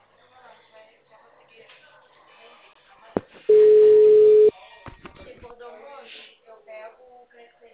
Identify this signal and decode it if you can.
Ringback tone (Europe, 425 Hz), with speech.
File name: out- -2045- - - .45244